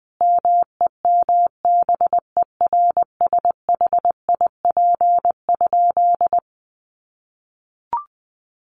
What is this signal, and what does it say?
Morse code, heard as a keyed tone, about 20 wpm, 700 Hz. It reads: MEMBERSHIP?
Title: Sentences from Top 1400 Words 20wpm